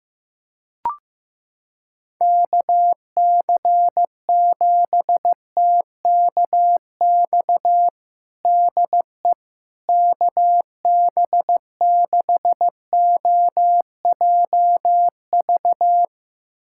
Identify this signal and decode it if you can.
Morse code, heard as a keyed tone, about 15 wpm, 700 Hz. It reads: KC7TKX DE KB6OJV
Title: USA Callsign DE Callsign - 6 Characters - 15wpm